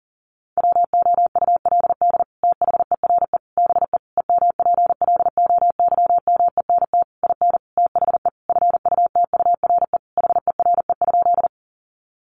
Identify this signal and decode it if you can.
Morse code, heard as a keyed tone, about 40 wpm, 700 Hz. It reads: WOULD THERE BE EMPLOYMENT IN THE FUTURE HERE?
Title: Sentences from Top 1500 Words 40wpm